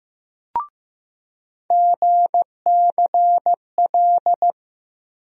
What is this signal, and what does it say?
Morse code, heard as a keyed tone, about 15 wpm, 700 Hz. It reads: GCL
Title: USA Call Sign Postfixes - 3 Letters 15wpm